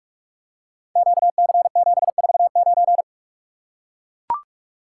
Morse code, heard as a keyed tone, about 44 wpm, 700 Hz. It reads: QC749
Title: Code Groups - 5 Letter:Numbers - 45wpm